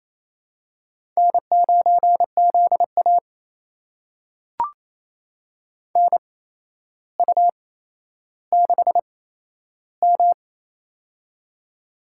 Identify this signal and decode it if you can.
Morse code, heard as a keyed tone, about 28 wpm, spaced as for 12 wpm, 700 Hz. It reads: N9ZA NU6M
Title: USA Call Signs - 4 Characters - 8x Character Spacing - 28wpm